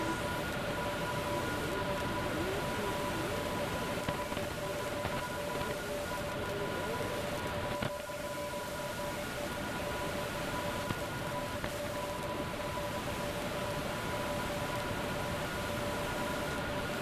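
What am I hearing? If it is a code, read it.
Two-tone teleprinter signal, about 850 Hz shift.
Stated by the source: Teleprinter signal N57K from a numbers station.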